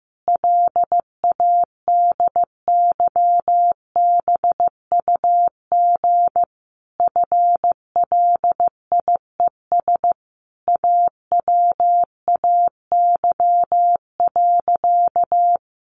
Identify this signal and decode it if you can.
Morse code, heard as a keyed tone, about 15 wpm, 700 Hz. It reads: LADYBUG FLIES AWAY.
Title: Additional Easy Phrases 15wpm